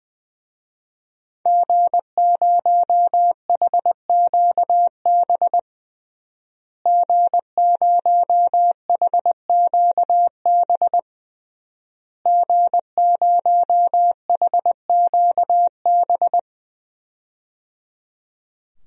Morse code, heard as a keyed tone, about 20 wpm, 700 Hz. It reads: G0HQB G0HQB G0HQB
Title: DXCC-100 Call Signs - 5 Characters - Repeated 3x - 20wpm